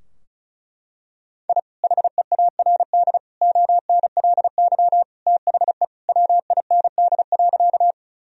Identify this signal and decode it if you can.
Morse code, heard as a keyed tone, about 34 wpm, 700 Hz. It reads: I HEARD ONLY THE WIND.
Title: Sentences from Top 1000 Words - Review Part 2 - 35wpm